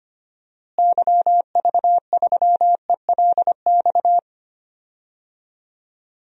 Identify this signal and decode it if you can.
Morse code, heard as a keyed tone, about 25 wpm, 700 Hz. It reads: YV3ELX
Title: DXCC-100 Call Signs - 6 Characters - Repeated 2x - 25wpm